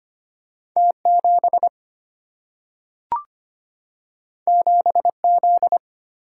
Morse code, heard as a keyed tone, about 25 wpm, 700 Hz. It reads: T 7 7 Z
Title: Single-Letter Single-Number 25wpm